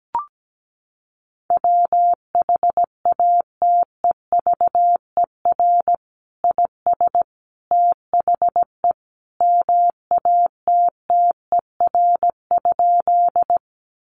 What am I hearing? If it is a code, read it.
Morse code, heard as a keyed tone, about 17 wpm, 700 Hz. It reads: WHATEVER IS THE MATTER?
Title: Sentences from Top 1100 Words 17wpm